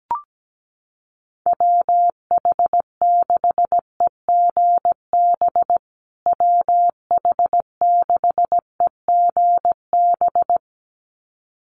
Morse code, heard as a keyed tone, about 17 wpm, 700 Hz. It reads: WH6EGB WH6EGB
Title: USA Call Signs Repeated 2x 17wpm